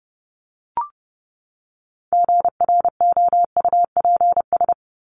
Morse code, heard as a keyed tone, about 30 wpm, 700 Hz. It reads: GROUPS
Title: ES:S Suffix - Top 100 Words with Base Words 30wpm